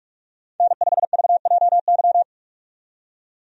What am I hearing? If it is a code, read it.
Morse code, heard as a keyed tone, about 44 wpm, 700 Hz. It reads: N5VJY
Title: USA Call Signs - 5 Characters - 45wpm